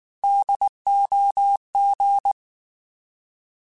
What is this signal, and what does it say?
Morse code, heard as a keyed tone, about 19 wpm, 800 Hz. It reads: DOG
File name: Alphabetical / Alphabetical mp3s / Ds @18